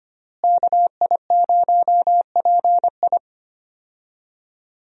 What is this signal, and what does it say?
Morse code, heard as a keyed tone, about 25 wpm, 700 Hz. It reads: KI0PI
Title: USA Call Signs - 5 Characters - Repeated 2x - 25wpm